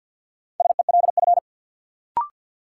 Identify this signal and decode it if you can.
Morse code, heard as a keyed tone, about 49 wpm, 700 Hz. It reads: SELF
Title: Sentences from Top 1400 Words - New Word Drill Alt - 50wpm